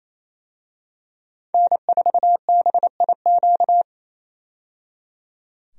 Morse code, heard as a keyed tone, about 28 wpm, 700 Hz. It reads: N4BIQ
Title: USA Call Signs - 5 Characters - 28wpm